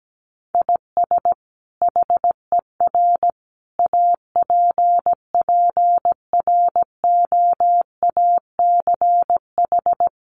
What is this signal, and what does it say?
Morse code, heard as a keyed tone, about 17 wpm, 700 Hz. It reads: IS HER APPROACH
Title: Sentences from Top 1500 Words 17wpm